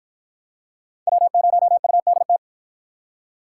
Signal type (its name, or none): Morse code